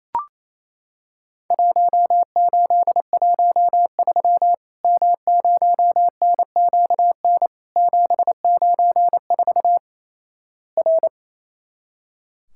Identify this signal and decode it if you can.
Morse code, heard as a keyed tone, about 28 wpm, 700 Hz. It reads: 1813 M0NQN 794 R
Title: Worked All Europe DX Contest - QTC Exchange - 28wpm